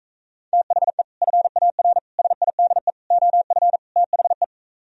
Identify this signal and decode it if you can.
Morse code, heard as a keyed tone, about 45 wpm, 700 Hz. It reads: THE FAR SIDE OF THE